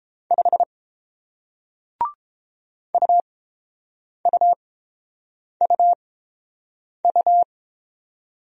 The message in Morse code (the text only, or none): 5 UUUU